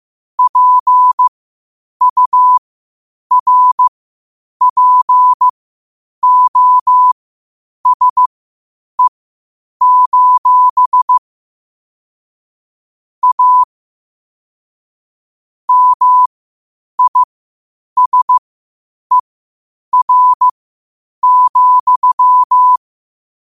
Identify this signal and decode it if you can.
Morse code, heard as a keyed tone, about 15 wpm, spaced as for 8 wpm, 1000 Hz. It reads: PURPOSE: A MISER,
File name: qotdfiles / 8wpm